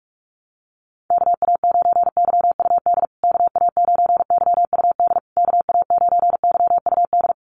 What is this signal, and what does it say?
Morse code, heard as a keyed tone, about 44 wpm, 690 Hz. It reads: KA9YUD KA9YUD KA9YUD
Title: USA Call Signs Repeated 3x 45wpm